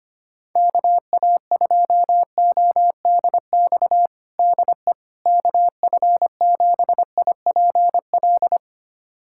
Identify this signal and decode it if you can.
Morse code, heard as a keyed tone, about 25 wpm, 700 Hz. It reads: KA2ODX DE KF7IPL
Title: USA Callsign DE Callsign - 6 Characters - 25wpm